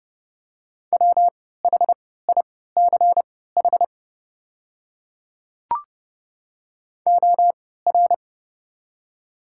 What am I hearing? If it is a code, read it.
Morse code, heard as a keyed tone, about 30 wpm, spaced as for 17 wpm, 700 Hz. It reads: WHICH OR
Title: 3x Character Spacing - Top 200 Words - 30wpm